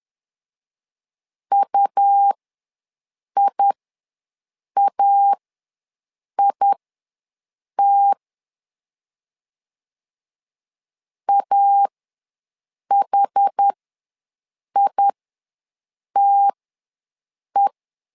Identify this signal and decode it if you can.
Morse code, heard as a keyed tone, about 10 wpm, spaced as for 6 wpm, 800 Hz. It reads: UIAIT AHITE
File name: LEZIONI / 2 UHDN / 2 UHD ws 10wpm 2058